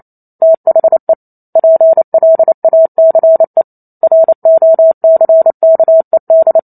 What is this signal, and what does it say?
Morse code, heard as a keyed tone, about 28 wpm, 650 Hz. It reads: THE PLACE ROCKED